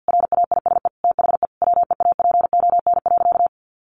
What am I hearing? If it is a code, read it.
Morse code, heard as a keyed tone, about 50 wpm, 700 Hz. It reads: RAISE THE WEAPON.
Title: A Sound Of Thunder 50wpm